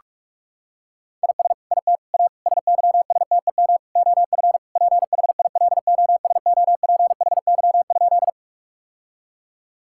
Morse code, heard as a keyed tone, about 46 wpm, 700 Hz. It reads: IS IT A SYSTEM OF PHILOSOPHY?
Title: Sentences from Top 1000 Words - Review Part 6 - 45wpm